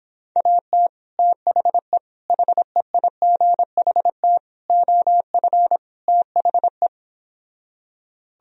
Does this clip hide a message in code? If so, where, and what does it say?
Yes: Morse code from 0.4 to 6.9 s, heard as a keyed tone, about 26 wpm, 700 Hz. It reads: AT THE HEIGHT OF THE